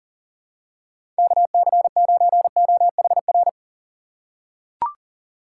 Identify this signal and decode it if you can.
Morse code, heard as a keyed tone, about 38 wpm, 700 Hz. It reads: KC9OHR
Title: USA Call Signs - 6 Characters - 2x Character Spacing - 40wpm